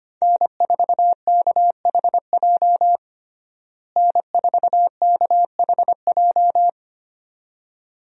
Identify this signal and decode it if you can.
Morse code, heard as a keyed tone, about 25 wpm, 700 Hz. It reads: N4KHJ N4KHJ
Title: USA Call Signs - 5 Characters - Repeated 2x - 25wpm